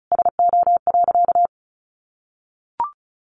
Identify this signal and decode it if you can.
Morse code, heard as a keyed tone, about 34 wpm, 700 Hz. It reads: SO.
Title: Sentences from Top 1300 Words 35wpm